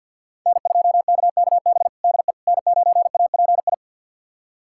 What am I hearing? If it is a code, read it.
Morse code, heard as a keyed tone, about 51 wpm, 700 Hz. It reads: N2KKB DE N9API
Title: USA Callsign DE Callsign - 5 Characters - 50wpm